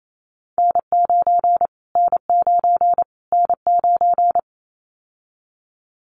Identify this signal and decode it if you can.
Morse code, heard as a keyed tone, about 28 wpm, 700 Hz. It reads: N9 N9 N9
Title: Callsign Prefixes for Top 25 DXCC - Repeated 3x - 28wpm